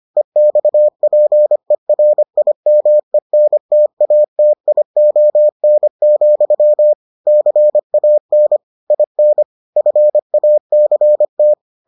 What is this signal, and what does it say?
Morse code, heard as a keyed tone, about 25 wpm, 600 Hz. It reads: EXPERIMENTATION, CAN IN FACT